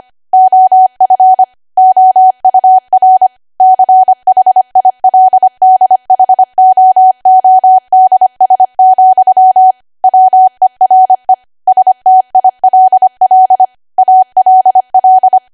Morse code, heard as a keyed tone, about 25 wpm, 750 Hz. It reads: OF OUR CHILDHOODS, WERE STILL ALL
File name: Archive / 25 WPM / 25WPM